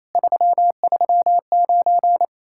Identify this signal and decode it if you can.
Morse code, heard as a keyed tone, about 26 wpm, 700 Hz. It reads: 339
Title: Speed-Racing Common QSO Phrases Encore - 28wpm